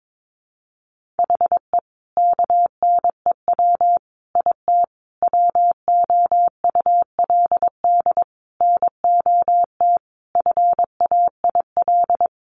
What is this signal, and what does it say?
Morse code, heard as a keyed tone, about 22 wpm, 700 Hz. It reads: HE KNEW IT WOULD NOT FAIL
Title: Sentences from Top 1100 Words 22wpm